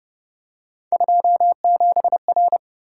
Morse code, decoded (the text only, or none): 27R